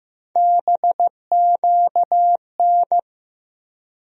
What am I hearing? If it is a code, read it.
Morse code, heard as a keyed tone, about 16 wpm, 700 Hz. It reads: BQN
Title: USA Call Sign Postfixes - 3 Letters 15wpm